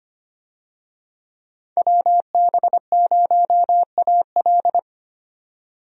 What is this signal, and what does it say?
Morse code, heard as a keyed tone, about 25 wpm, 700 Hz. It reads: WB0AL
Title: USA Call Signs - 5 Characters - 25wpm